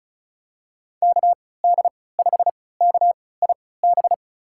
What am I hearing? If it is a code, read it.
Morse code, heard as a keyed tone, about 36 wpm, spaced as for 20 wpm, 700 Hz. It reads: KD5KIB
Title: USA Call Signs - 6 Characters - 3x Character Spacing - 35wpm